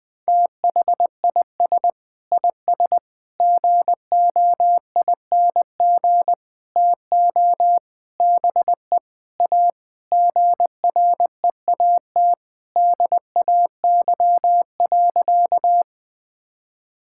Morse code, heard as a keyed tone, about 20 wpm, 700 Hz. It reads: THIS IS GOING TO BE A GREAT DAY.
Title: Sentences with 6-Word Phrases - 20wpm